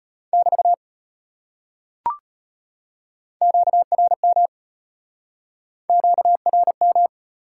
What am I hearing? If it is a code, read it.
Morse code, heard as a keyed tone, about 33 wpm, 700 Hz. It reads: <BT> QRM QRM